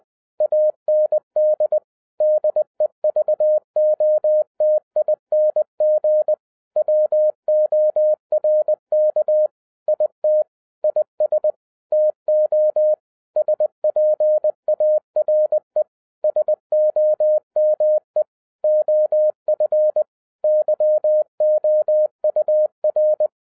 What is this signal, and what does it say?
Morse code, heard as a keyed tone, about 20 wpm, 610 Hz. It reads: AND DEVOTING WORK IT IS TO SPARE SOME OF YOUR
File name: 2012-01-mp3 / n16-THROUGH EIGHT MONTHS-20wpm